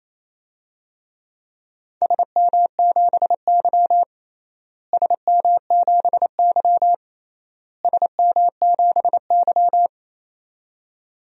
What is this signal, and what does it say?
Morse code, heard as a keyed tone, about 28 wpm, 700 Hz. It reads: SM7Y SM7Y SM7Y
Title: DXCC-100 Call Signs - 4 Characters - Repeated 3x - 28wpm